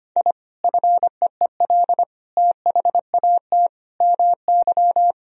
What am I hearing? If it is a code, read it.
Morse code, heard as a keyed tone, about 25 wpm, 700 Hz. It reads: I FEEL THAT MY
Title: Sentences from Top 1000 Words - Review Part 1 - 25wpm